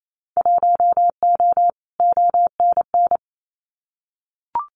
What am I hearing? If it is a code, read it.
Morse code, heard as a keyed tone, about 28 wpm, 700 Hz. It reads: 1O ONN
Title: Winter Field Day Exchanges Encore 28wpm